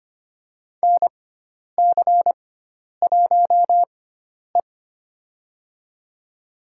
Morse code, heard as a keyed tone, about 25 wpm, spaced as for 10 wpm, 700 Hz. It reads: NC1E
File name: Lesson 16g - Letter c - Call Signs 25wpm